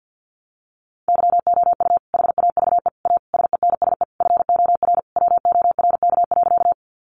Morse code, heard as a keyed tone, about 50 wpm, 700 Hz. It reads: YOU HAVE A SENSE FOR WORK.
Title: Sentences from Top 700 Words 50wpm